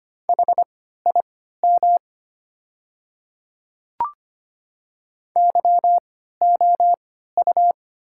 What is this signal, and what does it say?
Morse code, heard as a keyed tone, about 25 wpm, spaced as for 14 wpm, 700 Hz. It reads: HIM YOU
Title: 3x Character Spacing - Top 200 Words - 25wpm